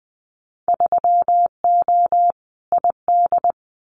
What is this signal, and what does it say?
Morse code, heard as a keyed tone, about 20 wpm, 700 Hz. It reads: 3O ID
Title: Winter Field Day Exchanges Encore 20wpm